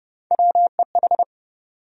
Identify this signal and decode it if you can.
Morse code, heard as a keyed tone, about 30 wpm, 700 Hz. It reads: WEH
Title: USA Call Sign Postfixes - 3 Letters 30wpm